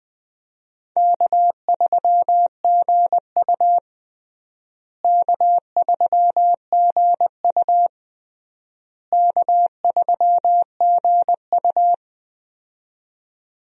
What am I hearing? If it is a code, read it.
Morse code, heard as a keyed tone, about 21 wpm, 700 Hz. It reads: K3GU K3GU K3GU
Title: USA Call Signs - 4 Characters - Repeated 3x - 20wpm